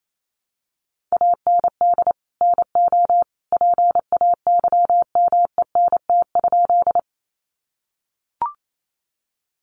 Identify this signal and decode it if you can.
Morse code, heard as a keyed tone, about 28 wpm, 700 Hz. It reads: AND NO PAYMENT?
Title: Sentences from Top 1200 Words 28wpm